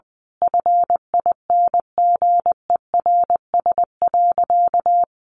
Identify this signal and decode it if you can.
Morse code, heard as a keyed tone, about 20 wpm, 700 Hz. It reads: FINGERS.
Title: A Sound Of Thunder 20wpm